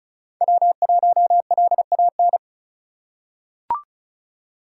Morse code, heard as a keyed tone, about 36 wpm, 700 Hz. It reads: W1LAN
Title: USA Call Signs - 5 Characters - 6x Character Spacing - 35wpm